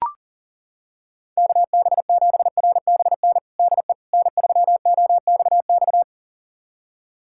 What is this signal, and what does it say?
Morse code, heard as a keyed tone, about 40 wpm, 700 Hz. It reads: KB7RBN DE N3OXX
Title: USA Callsign DE Callsign 40wpm